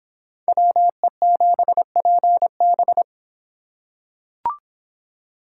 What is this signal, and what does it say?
Morse code, heard as a keyed tone, about 24 wpm, 700 Hz. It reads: WE7PB